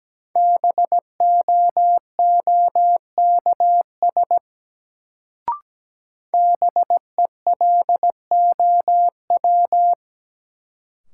Morse code, heard as a keyed tone, about 17 wpm, 700 Hz. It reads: BOOKS BELOW
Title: Sentences from Top 1200 Words - New Word Drill Alt - 17wpm